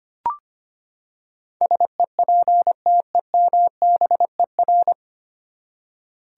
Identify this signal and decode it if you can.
Morse code, heard as a keyed tone, about 25 wpm, 700 Hz. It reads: SEPTEMBER
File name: Nine Letter Words - 25wpm